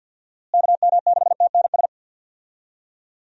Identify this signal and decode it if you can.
Morse code, heard as a keyed tone, about 53 wpm, 700 Hz. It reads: KM6TNS
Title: USA Call Signs - 6 Characters - Repeated 2x - 50wpm